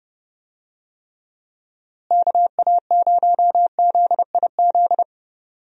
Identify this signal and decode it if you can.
Morse code, heard as a keyed tone, about 30 wpm, 700 Hz. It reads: KA0ZIZ
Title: USA Call Signs - 6 Characters - Repeated 3x - 30wpm